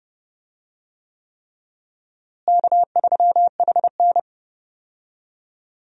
Morse code, heard as a keyed tone, about 30 wpm, 700 Hz. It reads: K3HN
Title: USA Call Signs - 4 Characters - Repeated 3x - 30wpm